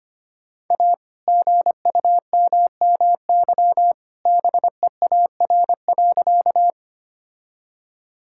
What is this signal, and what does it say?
Morse code, heard as a keyed tone, about 25 wpm, 700 Hz. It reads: A GUMMY BEAR.
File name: practice / 25wpm / All-Age Appropriate Jokes - 25wpm